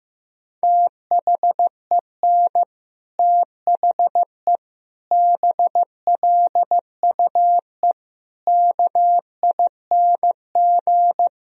Morse code, heard as a keyed tone, about 15 wpm, 700 Hz. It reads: THEN THE BLUE KING
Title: Sentences from Top 1000 Words - Review Part 3 - 15wpm